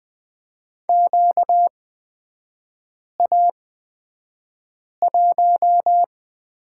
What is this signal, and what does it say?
Morse code, heard as a keyed tone, about 20 wpm, spaced as for 5 wpm, 690 Hz. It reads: QA1
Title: 236 Lesson 38e - Letter q - Three Characters 20wpm